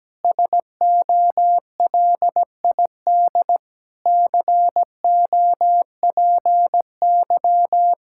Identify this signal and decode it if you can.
Morse code, heard as a keyed tone, about 17 wpm, 700 Hz. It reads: SOLID COPY
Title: Practice QSO - Exchange 3 - More Info - 17wpm